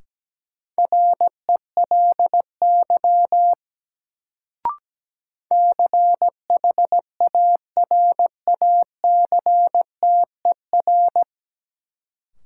Morse code, heard as a keyed tone, about 17 wpm, 700 Hz. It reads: RELY CHARACTER
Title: Sentences from Top 1200 Words - New Word Drill - 17wpm